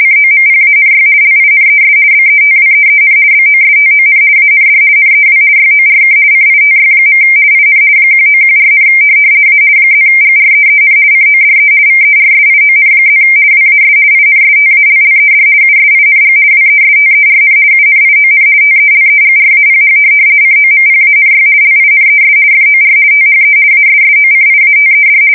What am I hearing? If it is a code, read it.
RTTY, about 50 baud. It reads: COMPANY LINES BETWEEN BOSTON AND NEW YORK CITY. IN 1914, A +GREEN CODE VERSION+ WAS INSTALLED USING WESTERN UNION TELEGRAPH COMPANY LINES FOR THE ASSOCIATED
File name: fsk170-50B / teletype-corp